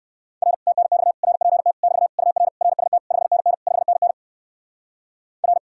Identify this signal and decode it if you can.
Morse code, heard as a keyed tone, about 58 wpm, 700 Hz. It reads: R TNX RPT <BT> UR RST 5NN 5NN R